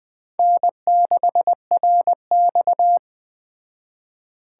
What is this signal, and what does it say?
Morse code, heard as a keyed tone, about 20 wpm, 700 Hz. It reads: N6RX
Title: Speed Racing - USA Call Signs 17wpm